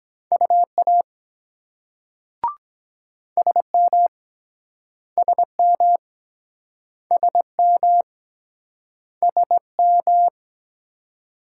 Morse code, heard as a keyed tone, about 24 wpm, 700 Hz. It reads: UA SM SM SM SM